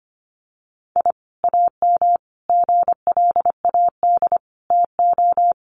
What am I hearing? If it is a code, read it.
Morse code, heard as a keyed tone, about 25 wpm, 700 Hz. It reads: I AM GLAD TO
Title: Sentences from Top 1300 Words 25wpm